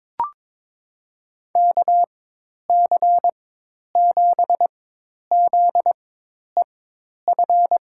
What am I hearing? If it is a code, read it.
Morse code, heard as a keyed tone, about 21 wpm, spaced as for 10 wpm, 700 Hz. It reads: KC7ZEF
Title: USA Call Signs - 6 Characters - 4x Character Spacing - 22wpm